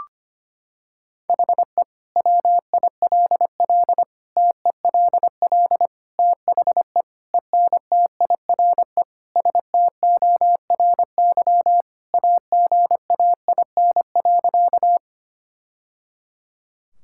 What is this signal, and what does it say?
Morse code, heard as a keyed tone, about 25 wpm, 700 Hz. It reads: HE WILL TELL THE ENTIRE STORY AGAIN.
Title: Sentences from Top 1000 Words - Review Part 5 - 25wpm